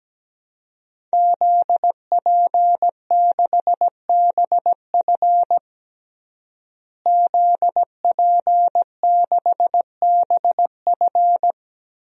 Morse code, heard as a keyed tone, about 17 wpm, 700 Hz. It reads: ZP6BF ZP6BF
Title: DXCC-100 Call Signs - 5 Characters - Repeated 3x - 17wpm